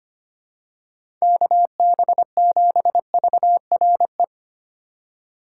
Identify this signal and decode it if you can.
Morse code, heard as a keyed tone, about 25 wpm, 700 Hz. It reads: KB7VRE
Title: USA Call Signs - 6 Characters - 4x Character Spacing - 25wpm